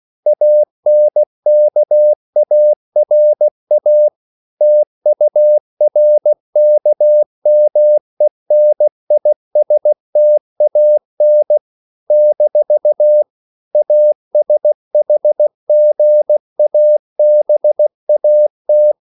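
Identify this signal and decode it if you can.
Morse code, heard as a keyed tone, about 16 wpm, 600 Hz. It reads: ANKARA TURKMENISTAN - ASHGABAT
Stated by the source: CW practice of national capitals at 16 words per minute, from ON6ZQ.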